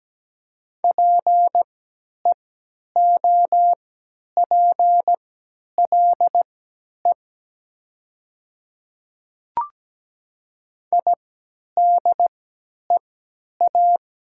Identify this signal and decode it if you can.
Morse code, heard as a keyed tone, about 17 wpm, spaced as for 10 wpm, 700 Hz. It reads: PEOPLE IDEA
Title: 3x Character Spacing - Top 300 Words - 17wpm